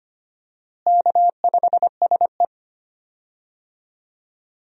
Morse code, heard as a keyed tone, about 25 wpm, 700 Hz. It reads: K5SE
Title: USA Call Signs - 4 Characters - Repeated 3x - 25wpm